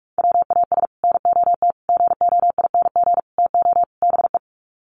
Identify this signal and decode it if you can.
Morse code, heard as a keyed tone, about 44 wpm, 700 Hz. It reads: WAS NOT GOING TO BE
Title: Sets of 5 Words Encore 45wpm